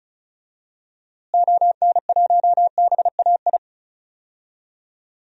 Morse code, heard as a keyed tone, about 36 wpm, 700 Hz. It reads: ON1BAI
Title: DXCC-100 Call Signs - 6 Characters - Repeated 2x - 35wpm